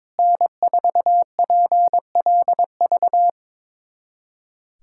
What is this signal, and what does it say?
Morse code, heard as a keyed tone, about 21 wpm, 700 Hz. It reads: N4PLV
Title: USA Callsign DE Callsign - 5 Characters - 22wpm